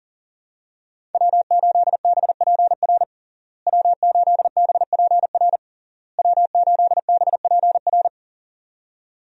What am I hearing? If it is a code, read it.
Morse code, heard as a keyed tone, about 40 wpm, 700 Hz. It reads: W8BPR W8BPR W8BPR
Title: USA Call Signs - 5 Characters - Repeated 3x - 40wpm